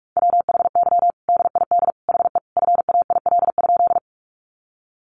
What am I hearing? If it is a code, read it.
Morse code, heard as a keyed tone, about 44 wpm, 700 Hz. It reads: WHY DID HE FAIL?
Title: Sentences from Top 1100 Words 45wpm